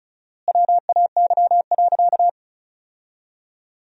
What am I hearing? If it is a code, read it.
Morse code, heard as a keyed tone, about 36 wpm, 700 Hz. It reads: WAY.
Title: 4x Word-Spacing - Sentences from Top 100 Words - 35wpm